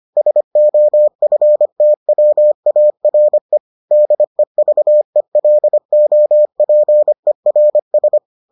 Morse code, heard as a keyed tone, about 25 wpm, 600 Hz. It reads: SOFTWARE DEVELOPERS